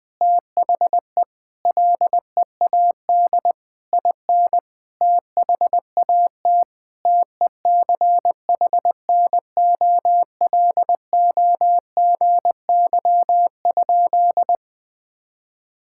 Morse code, heard as a keyed tone, about 20 wpm, 700 Hz. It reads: THE LEAD IN THAT TECHNOLOGY?
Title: Sentences from Top 1000 Words - Review Part 5 - 20wpm